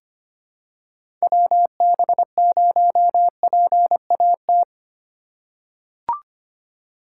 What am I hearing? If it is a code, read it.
Morse code, heard as a keyed tone, about 25 wpm, 700 Hz. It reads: WB0PAT
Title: USA Call Signs - 6 Characters - 3x Character Spacing - 25wpm